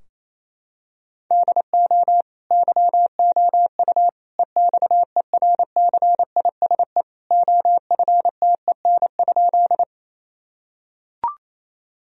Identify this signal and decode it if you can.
Morse code, heard as a keyed tone, about 28 wpm, 700 Hz. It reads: DO YOU EXERCISE OFTEN?
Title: Sentences from Top 1000 Words - Review Part 4 - 28wpm